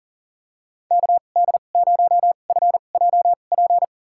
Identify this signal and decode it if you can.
Morse code, heard as a keyed tone, about 40 wpm, spaced as for 29 wpm, 700 Hz. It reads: KD0FJP
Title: USA Call Signs - 6 Characters - 2x Character Spacing - 40wpm